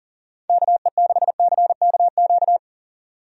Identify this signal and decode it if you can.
Morse code, heard as a keyed tone, about 40 wpm, 700 Hz. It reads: KE6CKQ
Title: USA Call Signs - 6 Characters - 40wpm